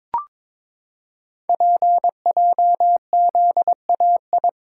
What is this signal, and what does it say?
Morse code, heard as a keyed tone, about 22 wpm, 700 Hz. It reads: PJZAI
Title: Code Groups - 5 Letters - 22wpm